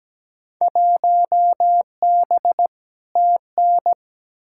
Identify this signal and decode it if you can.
Morse code, heard as a keyed tone, about 17 wpm, 700 Hz. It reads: 1B TN
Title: Field Day Exchanges - 17wpm